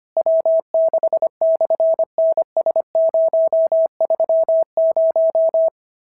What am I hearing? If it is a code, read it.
Morse code, heard as a keyed tone, about 24 wpm, 650 Hz. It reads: W6/NS030
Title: SOTA QSO Practice 25wpm